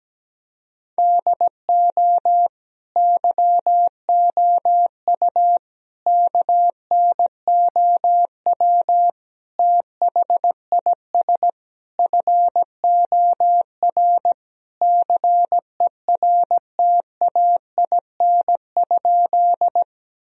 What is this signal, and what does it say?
Morse code, heard as a keyed tone, about 17 wpm, 700 Hz. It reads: DO YOU KNOW THIS FOR CERTAIN?
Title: Sentences from Top 1000 Words - Review Part 2 - 17wpm